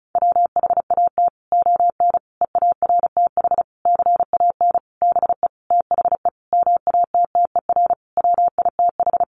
Morse code, heard as a keyed tone, about 34 wpm, 700 Hz. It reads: WHAT ON EARTH CAN BE THE MATTER WITH
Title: Sentences from Top 700 Words 35wpm